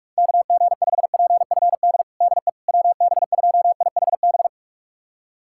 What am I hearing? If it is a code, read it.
Morse code, heard as a keyed tone, about 42 wpm, 700 Hz. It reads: KG5PFD DE WB2IHB
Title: USA Callsign DE Callsign 45wpm